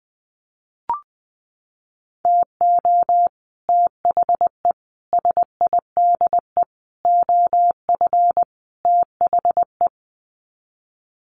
Morse code, heard as a keyed tone, about 20 wpm, 700 Hz. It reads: TO THE SIDE OF THE
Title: Sets of 5 Words 20wpm